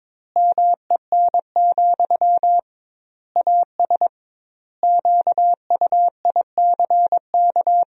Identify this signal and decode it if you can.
Morse code, heard as a keyed tone, about 22 wpm, 700 Hz. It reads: MEN, AS QUICK
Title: 2x Word-Spacing - Sentences from Top 500 Words - 22wpm